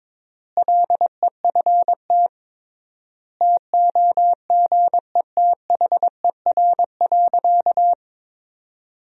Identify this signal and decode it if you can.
Morse code, heard as a keyed tone, about 22 wpm, 700 Hz. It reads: LEFT TOGETHER.
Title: 3x Word-Spacing - Sentences from Top 400 Words - 22wpm